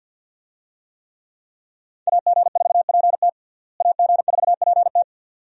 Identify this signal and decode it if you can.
Morse code, heard as a keyed tone, about 50 wpm, 700 Hz. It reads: AG4PT AG4PT
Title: USA Call Signs - 5 Characters - Repeated 2x - 50wpm